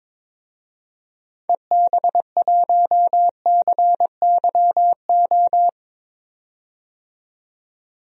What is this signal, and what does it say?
Morse code, heard as a keyed tone, about 21 wpm, 700 Hz. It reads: EB1CYO
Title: DXCC-100 Call Signs - 6 Characters - 22wpm